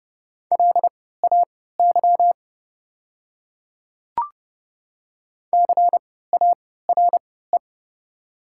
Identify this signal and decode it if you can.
Morse code, heard as a keyed tone, about 30 wpm, spaced as for 17 wpm, 700 Hz. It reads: LAY CARE
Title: 3x Character Spacing - Top 500 Words - 30wpm